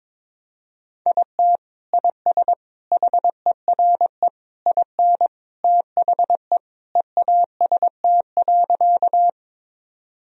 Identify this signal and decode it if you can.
Morse code, heard as a keyed tone, about 22 wpm, 700 Hz. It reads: IT IS HERE IN THE EAST.
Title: Sentences from Top 1000 Words - Review Part 3 - 22wpm